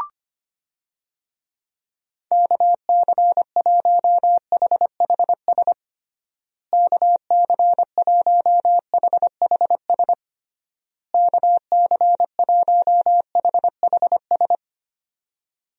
Morse code, heard as a keyed tone, about 25 wpm, 700 Hz. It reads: KC1HHS KC1HHS KC1HHS
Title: USA Call Signs - 6 Characters - Repeated 3x - 25wpm